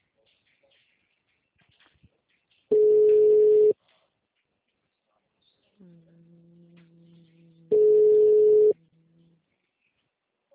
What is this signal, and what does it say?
Ringback tone (Europe, 425 Hz).
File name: out- -2005- - - .13231